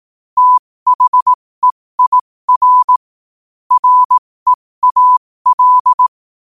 Morse code, heard as a keyed tone, about 18 wpm, spaced as for 15 wpm, 1000 Hz. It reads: THEIR REAL
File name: qotdfiles / 15wpm